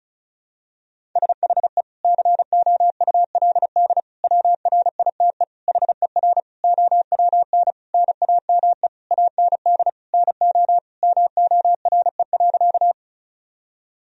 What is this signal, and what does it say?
Morse code, heard as a keyed tone, about 36 wpm, 700 Hz. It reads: SHE COULD WRITE HER OWN NAME AND NO MORE.
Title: Sentences from Top 1000 Words - Review Part 1 - 35wpm